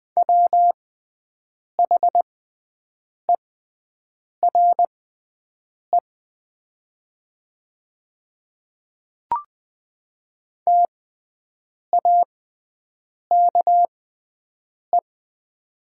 Morse code, heard as a keyed tone, about 20 wpm, spaced as for 7 wpm, 700 Hz. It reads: WHERE TAKE
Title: 6x Character Spacing - Top 200 Words - 20wpm